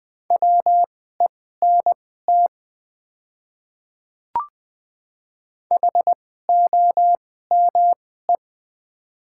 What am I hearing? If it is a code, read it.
Morse code, heard as a keyed tone, about 20 wpm, spaced as for 14 wpm, 700 Hz. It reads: WENT HOME
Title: 2x Character Spacing - Top 200 Words - 20wpm